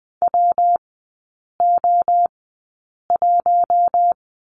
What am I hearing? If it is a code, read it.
Morse code, heard as a keyed tone, about 20 wpm, spaced as for 8 wpm, 700 Hz. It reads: WO1
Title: 114 Lesson 19e - Letter w - Three Characters 20wpm